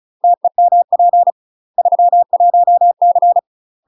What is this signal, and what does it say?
Morse code, heard as a keyed tone, about 36 wpm, 700 Hz. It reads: TEMP 31C
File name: UK MASTER QSOs TRAINING / 35WPM / UK MASTER QSOs 0007